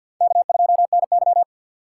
Morse code, heard as a keyed tone, about 61 wpm, 700 Hz. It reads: K2NY
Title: Speed Racing - USA 1x2 Call Signs 50wpm